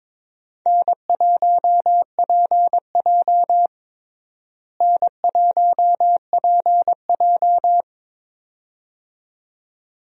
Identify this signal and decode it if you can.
Morse code, heard as a keyed tone, about 22 wpm, 700 Hz. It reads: N1PJ N1PJ
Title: USA Call Signs - 4 Characters - Repeated 2x - 22wpm